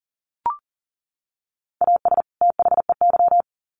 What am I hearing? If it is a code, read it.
Morse code, heard as a keyed tone, about 40 wpm, 700 Hz. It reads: AS THEY
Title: Sets of 2 Words 40wpm